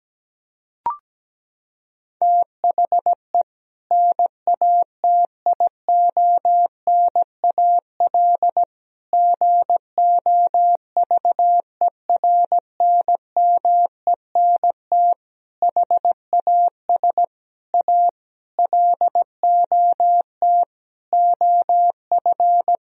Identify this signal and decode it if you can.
Morse code, heard as a keyed tone, about 17 wpm, 700 Hz. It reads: THE NATIONAL GOVERNMENT HAS A LOT OF
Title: Sentences from Top 1000 Words - Review Part 4 - 17wpm